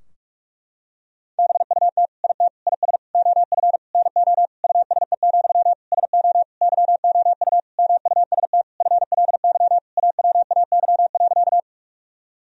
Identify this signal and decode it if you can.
Morse code, heard as a keyed tone, about 46 wpm, 700 Hz. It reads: BUT IT IS OF NO USE, SO YOU MUST FLY AWAY.
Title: Sentences from Top 1000 Words - Review Part 2 - 45wpm